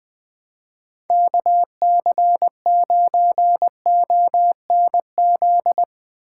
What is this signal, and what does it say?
Morse code, heard as a keyed tone, about 20 wpm, 700 Hz. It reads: KC9ONZ
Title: Speed Racing - USA 2x3 Call Signs 20wpm